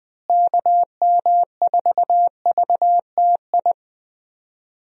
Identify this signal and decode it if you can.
Morse code, heard as a keyed tone, about 21 wpm, 700 Hz. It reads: KM4VTI
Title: USA Call Signs - 6 Characters - Repeated 2x - 20wpm